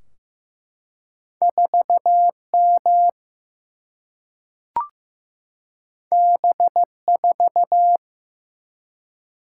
Morse code, heard as a keyed tone, about 15 wpm, 700 Hz. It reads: 4M B4
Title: Single-Letter Single-Number 15wpm